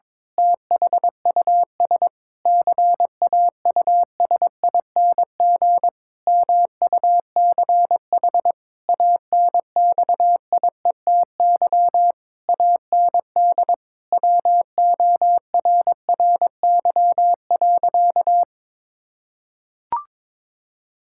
Morse code, heard as a keyed tone, about 22 wpm, 700 Hz. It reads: THUS CAUSING MUCH ANXIETY AND WORRY.
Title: Pangram Challenge v2 - 22wpm